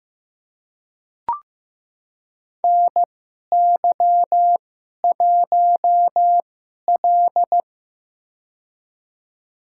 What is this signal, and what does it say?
Morse code, heard as a keyed tone, about 15 wpm, spaced as for 11 wpm, 690 Hz. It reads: NY1L
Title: USA Call Signs - 4 Characters - 2x Character Spacing - 15wpm